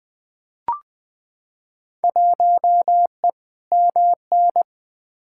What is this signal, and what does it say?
Morse code, heard as a keyed tone, about 20 wpm, 700 Hz. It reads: 1E MN
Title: Field Day Exchanges - 20wpm